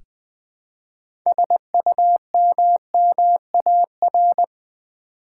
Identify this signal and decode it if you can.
Morse code, heard as a keyed tone, about 20 wpm, 700 Hz. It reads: SUMMAR
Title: Build A Word - 3 Letters At A Time - 9-Letter Words - 20wpm